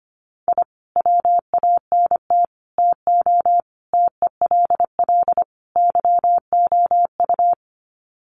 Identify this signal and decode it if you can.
Morse code, heard as a keyed tone, about 25 wpm, 700 Hz. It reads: I WANT TO TELL YOU
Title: Sets of 5 Words Encore 25wpm